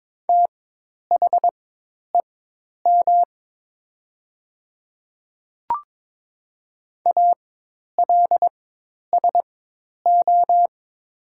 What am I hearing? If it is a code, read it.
Morse code, heard as a keyed tone, about 22 wpm, spaced as for 10 wpm, 700 Hz. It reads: THEM ALSO
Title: 4x Character Spacing - Top 300 Words - 22wpm